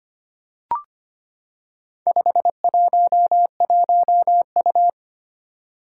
Morse code, heard as a keyed tone, about 25 wpm, 700 Hz. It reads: 511U
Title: Four Challenging Letter-Number Combinations 25wpm